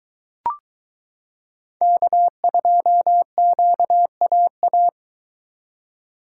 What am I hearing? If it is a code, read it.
Morse code, heard as a keyed tone, about 22 wpm, 700 Hz. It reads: K2QAA